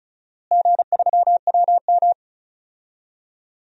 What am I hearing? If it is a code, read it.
Morse code, heard as a keyed tone, about 34 wpm, 700 Hz. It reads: G3WM
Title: DXCC-100 Call Signs - 4 Characters - Repeated 2x - 35wpm